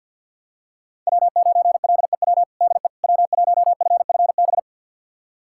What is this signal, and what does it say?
Morse code, heard as a keyed tone, about 52 wpm, 700 Hz. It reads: W9LEW DE W1FFB
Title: USA Callsign DE Callsign - 5 Characters - 50wpm